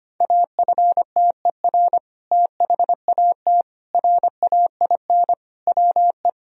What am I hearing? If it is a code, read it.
Morse code, heard as a keyed tone, about 25 wpm, 700 Hz. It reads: AFTER THAT RAIN WE
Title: Sentences from Top 1000 Words - Review Part 1 - 25wpm